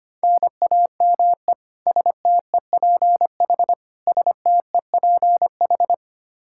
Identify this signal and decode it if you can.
Morse code, heard as a keyed tone, about 25 wpm, 700 Hz. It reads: NAME STEPH STEPH
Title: Practice QSO - Exchange 1 - Beginning - 25wpm